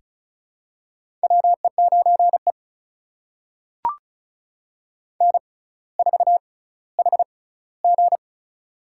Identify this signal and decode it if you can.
Morse code, heard as a keyed tone, about 36 wpm, spaced as for 18 wpm, 700 Hz. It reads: WE9E N4HG
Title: USA Call Signs - 4 Characters - 6x Character Spacing - 35wpm